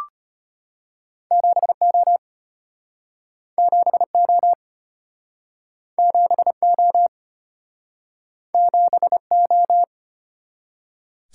Morse code, heard as a keyed tone, about 33 wpm, 700 Hz. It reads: 7O 7O 7O 7O